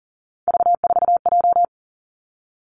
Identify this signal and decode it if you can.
Morse code, heard as a keyed tone, about 43 wpm, 700 Hz. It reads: V4J
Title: Three Challenging Letter-Number Combinations 40wpm